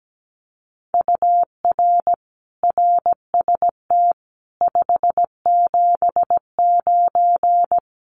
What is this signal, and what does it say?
Morse code, heard as a keyed tone, about 17 wpm, 700 Hz. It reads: UR RST 579
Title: Common QSO Phrases Encore 17wpm